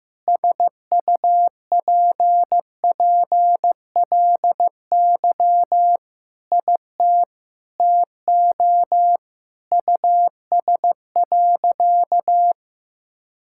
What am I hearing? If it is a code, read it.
Morse code, heard as a keyed tone, about 15 wpm, 700 Hz. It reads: SUPPLY IT TO US.
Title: Sentences from Top 1000 Words - Review Part 5 - 15wpm